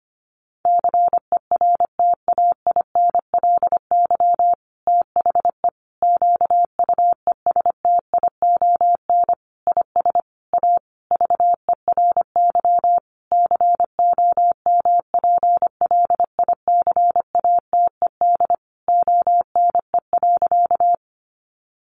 Morse code, heard as a keyed tone, about 25 wpm, 700 Hz. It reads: CERTAINLY THE QUESTION IS A VERY COMPLICATED ONE.
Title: Sentences from Top 1200 Words 25wpm